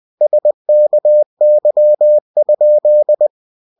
Morse code, heard as a keyed tone, about 20 wpm, 600 Hz. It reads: SKY?
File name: cw-books / winnie-the-pooh / winnie-20wpm-page-0006